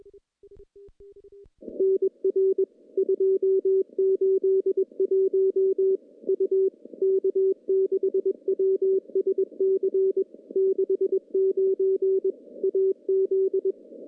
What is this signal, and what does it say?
Morse code, heard as a keyed tone, about 21 wpm, 380 Hz. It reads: NR 281 U K6WSC 69 AZ
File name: CW-SS-k5tr